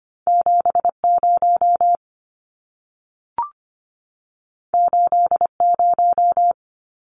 Morse code, heard as a keyed tone, about 25 wpm, 700 Hz. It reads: 70 80
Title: Numbers - 2-Digit 25wpm